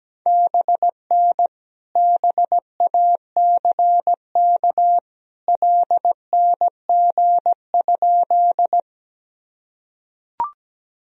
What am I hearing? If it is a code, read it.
Morse code, heard as a keyed tone, about 17 wpm, 700 Hz. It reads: BN BACK LNG?
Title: Sentences from Top 200 Words and Abbreviations 17wpm